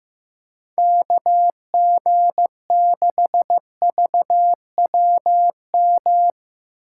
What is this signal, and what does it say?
Morse code, heard as a keyed tone, about 15 wpm, 700 Hz. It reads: KG6VWM
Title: USA Call Signs - 6 Characters - 6x Character Spacing - 15wpm